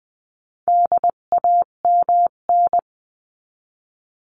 Morse code, heard as a keyed tone, about 20 wpm, 700 Hz. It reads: DAMN
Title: 258 Lesson 42c - Words 16-20wpm